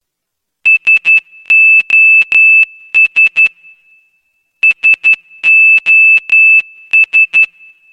Morse code, heard as a keyed tone, about 12 wpm, 2650 Hz. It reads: SOS SOS